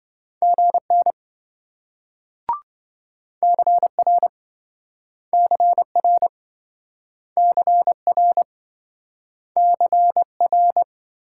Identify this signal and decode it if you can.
Morse code, heard as a keyed tone, about 28 wpm, 700 Hz. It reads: GN CR CR CR CR